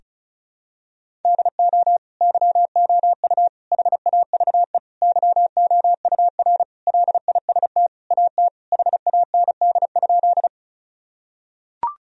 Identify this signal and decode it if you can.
Morse code, heard as a keyed tone, about 36 wpm, 700 Hz. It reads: DO YOU HAVE YOUR LIST AT HAND?
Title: Sentences from Top 1000 Words - Review Part 2 - 35wpm